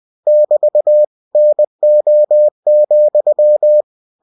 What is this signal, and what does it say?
Morse code, heard as a keyed tone, about 20 wpm, 600 Hz. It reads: <BT> NO,
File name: cw-books / winnie-the-pooh / winnie-20wpm-page-0065